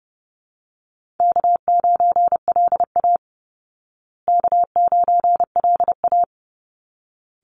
Morse code, heard as a keyed tone, about 30 wpm, 700 Hz. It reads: K9LA K9LA
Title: Easy Call Signs Repeated 2x - 30wpm